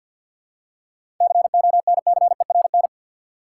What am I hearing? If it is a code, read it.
Morse code, heard as a keyed tone, about 52 wpm, 700 Hz. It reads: CONCERN
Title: Seven Letter Words - 50wpm